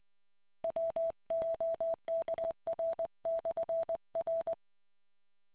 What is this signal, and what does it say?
Morse code, heard as a keyed tone, about 20 wpm, 660 Hz. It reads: WOKR/R